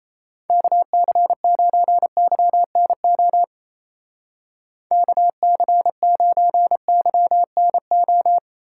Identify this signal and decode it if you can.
Morse code, heard as a keyed tone, about 34 wpm, 700 Hz. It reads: KC9YNO KC9YNO
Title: Speed Racing - USA 2x3 Call Signs 28wpm